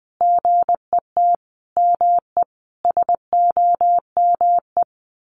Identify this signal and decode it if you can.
Morse code, heard as a keyed tone, about 19 wpm, 700 Hz. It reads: GET ME SOME
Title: Sentences from Top 1500 Words 20wpm